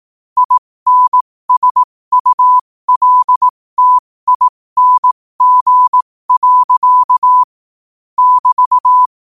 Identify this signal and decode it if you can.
Morse code, heard as a keyed tone, about 18 wpm, spaced as for 15 wpm, 1000 Hz. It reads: INSULTING. <BT>
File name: qotdfiles / 15wpm